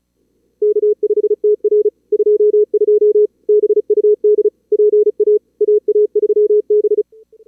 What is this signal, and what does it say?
Morse code, heard as a keyed tone, about 36 wpm, 440 Hz. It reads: K5TR 22 BUD PA AA3B
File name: CWSprint / Feb2022 / k5tr-feb-na-sprint-cw-k5ot -2300